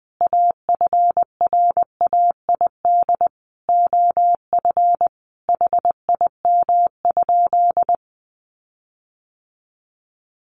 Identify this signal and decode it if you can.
Morse code, heard as a keyed tone, about 19 wpm, 700 Hz. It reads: AFRAID OF HIM?
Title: Sentences from Top 1500 Words 20wpm